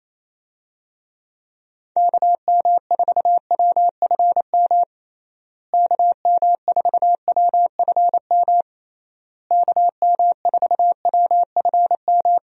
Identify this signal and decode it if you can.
Morse code, heard as a keyed tone, about 28 wpm, 700 Hz. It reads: KM4WFM KM4WFM KM4WFM
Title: USA Call Signs - 6 Characters - Repeated 3x - 28wpm